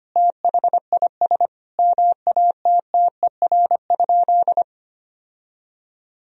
Morse code, heard as a keyed tone, about 25 wpm, 700 Hz. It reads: THIS MATTER?
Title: Sentences from Top 1000 Words - Review Part 1 - 25wpm